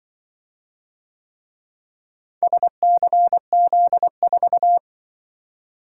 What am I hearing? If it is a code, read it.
Morse code, heard as a keyed tone, about 23 wpm, 700 Hz. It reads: SCZ4
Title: Speed-Racing - Four Letter-Number Combinations - 20wpm